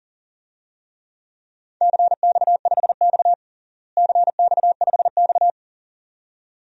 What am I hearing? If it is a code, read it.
Morse code, heard as a keyed tone, about 40 wpm, 700 Hz. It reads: CX5X CX5X
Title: DXCC-100 Call Signs - 4 Characters - Repeated 2x - 40wpm